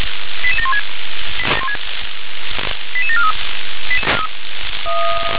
Five-tone selective call (ZVEI), heard as quick runs of five tones, tones of about 80 ms. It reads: 88716 88743